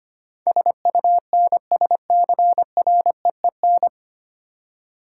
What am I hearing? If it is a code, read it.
Morse code, heard as a keyed tone, about 25 wpm, 700 Hz. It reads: SUNSCREEN
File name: practice / 25wpm / Easy Words with Hints - 25wpm